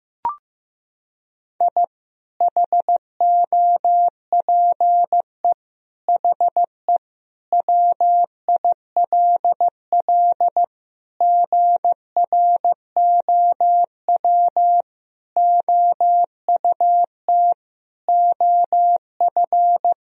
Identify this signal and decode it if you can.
Morse code, heard as a keyed tone, about 14 wpm, 700 Hz. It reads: I HOPE HE WILL GROW OUT OF
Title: Sentences from Top 1000 Words - Review Part 1 - 15wpm